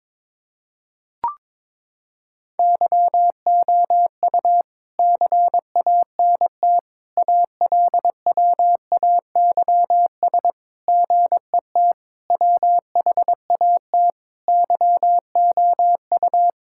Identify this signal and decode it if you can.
Morse code, heard as a keyed tone, about 22 wpm, 700 Hz. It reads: YOU CANT ALWAYS GET WHAT YOU
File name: practice / 22wpm / Sets of 7 Words - 22wpm